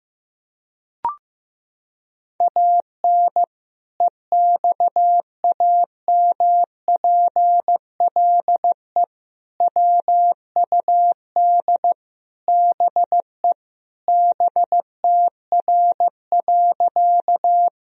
Morse code, heard as a keyed tone, about 16 wpm, 700 Hz. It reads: AN EXAMPLE WUD BE BTR.
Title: Sentences from Top 400 Words and Abbreviations 15wpm